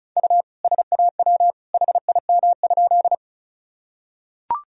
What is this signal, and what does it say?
Morse code, heard as a keyed tone, about 36 wpm, 700 Hz. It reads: U SAW HIM?
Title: Sentences from Top 300 Words and Abbreviations 35wpm